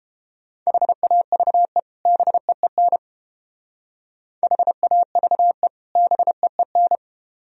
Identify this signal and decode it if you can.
Morse code, heard as a keyed tone, about 32 wpm, 700 Hz. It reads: HAVE BEEN HAVE BEEN